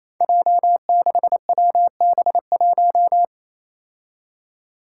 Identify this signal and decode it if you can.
Morse code, heard as a keyed tone, about 28 wpm, 700 Hz. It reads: J6WB1
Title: Five Challenging Letter-Number Combinations 28wpm